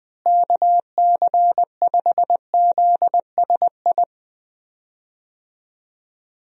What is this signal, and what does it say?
Morse code, heard as a keyed tone, about 20 wpm, 700 Hz. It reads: KC5ZSI
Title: USA Call Signs - 6 Characters - Repeated 2x - 20wpm